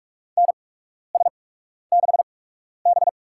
Morse code, heard as a keyed tone, about 47 wpm, spaced as for 12 wpm, 700 Hz. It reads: NS6B
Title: USA Call Signs - 4 Characters - 8x Character Spacing - 45wpm